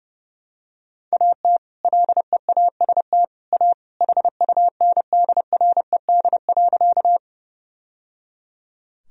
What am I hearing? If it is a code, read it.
Morse code, heard as a keyed tone, about 30 wpm, 700 Hz. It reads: AT LEAST A HUNDRED.
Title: Sentences from Top 1000 Words - Review Part 2 - 30wpm